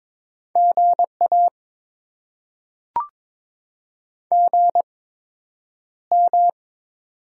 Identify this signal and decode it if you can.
Morse code, heard as a keyed tone, about 22 wpm, spaced as for 9 wpm, 700 Hz. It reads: GA GM
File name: Top 20 QSO Elements - 8x Character Spacing - 22wpm